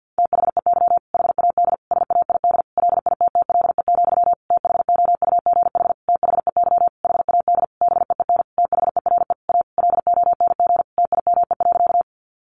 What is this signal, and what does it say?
Morse code, heard as a keyed tone, about 48 wpm, 700 Hz. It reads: THEY HAD SAID LITTLE, THOUGH THEY HAD BEEN THERE A LONG TIME.
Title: Sentences from Top 400 Words 50wpm